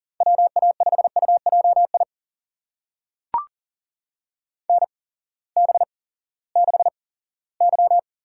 Morse code, heard as a keyed tone, about 40 wpm, spaced as for 20 wpm, 700 Hz. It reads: WA5UJI NB6Y
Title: USA Call Signs - 8x Character Spacing - 40wpm